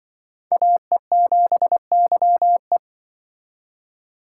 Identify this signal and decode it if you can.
Morse code, heard as a keyed tone, about 23 wpm, 700 Hz. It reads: AE7YE
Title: Speed Racing - USA 2x2 Call Signs 20wpm